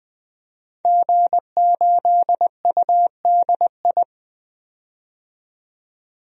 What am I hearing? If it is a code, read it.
Morse code, heard as a keyed tone, about 20 wpm, 700 Hz. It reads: G8UDI
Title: DXCC-100 Call Signs - 5 Characters - 20wpm